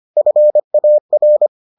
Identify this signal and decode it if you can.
Morse code, heard as a keyed tone, about 25 wpm, 600 Hz. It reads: FAR